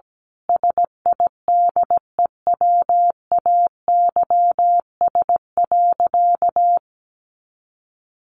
Morse code, heard as a keyed tone, about 17 wpm, 700 Hz. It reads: SIDEWAYS.
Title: A Sound Of Thunder 17wpm